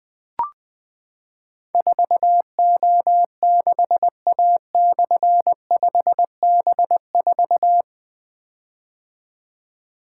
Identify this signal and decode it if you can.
Morse code, heard as a keyed tone, about 20 wpm, 700 Hz. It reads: 4O6A/5B4
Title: Difficult Call Signs 20wpm